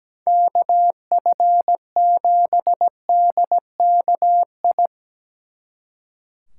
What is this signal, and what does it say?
Morse code, heard as a keyed tone, about 20 wpm, 700 Hz. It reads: KF7DKI
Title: Speed Racing - USA 2x3 Call Signs 17wpm